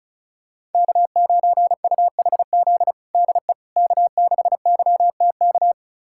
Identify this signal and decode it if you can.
Morse code, heard as a keyed tone, about 34 wpm, 700 Hz. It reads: K9UHZ DE K6YTK
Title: USA Callsign DE Callsign - 5 Characters - 35wpm